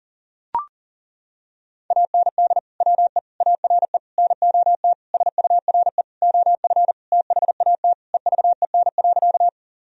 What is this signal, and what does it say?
Morse code, heard as a keyed tone, about 40 wpm, 700 Hz. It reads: AND WE ARE NOT SURE OF THAT EVEN.
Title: Sentences from Top 1000 Words - Review Part 1 - 40wpm